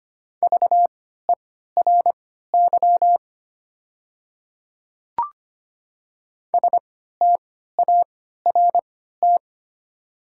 Morse code, heard as a keyed tone, about 25 wpm, spaced as for 14 wpm, 690 Hz. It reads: VERY START
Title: 3x Character Spacing - Top 300 Words - 25wpm